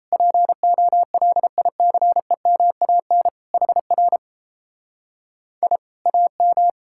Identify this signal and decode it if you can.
Morse code, heard as a keyed tone, about 38 wpm, 700 Hz. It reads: POLICEMAN HR I AM